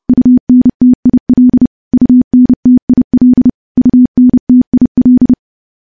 Morse code, heard as a keyed tone, about 30 wpm, 250 Hz. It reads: UNTIL UNTIL UNTIL
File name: morse / morse-training-30WPM